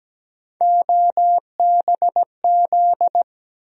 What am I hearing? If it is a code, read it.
Morse code, heard as a keyed tone, about 17 wpm, 700 Hz. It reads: OBZ
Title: USA Call Sign Postfixes - 3 Letters 17wpm